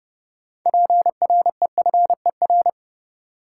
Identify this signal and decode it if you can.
Morse code, heard as a keyed tone, about 32 wpm, 700 Hz. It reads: PREFER
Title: Sentences from Top 1100 Words - New Word Drill - 30wpm